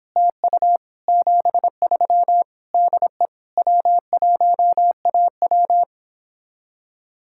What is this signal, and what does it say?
Morse code, heard as a keyed tone, about 26 wpm, 700 Hz. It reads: TU 73 DE W1AW